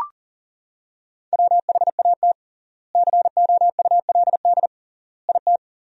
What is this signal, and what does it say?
Morse code, heard as a keyed tone, about 40 wpm, 700 Hz. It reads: WHAT COULD IT
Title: 3x Word-Spacing - Sentences from Top 100 Words - 40wpm